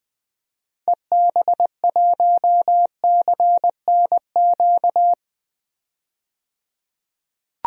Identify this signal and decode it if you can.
Morse code, heard as a keyed tone, about 20 wpm, 700 Hz. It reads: EB1CNQ
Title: DXCC-100 Call Signs - 6 Characters - 20wpm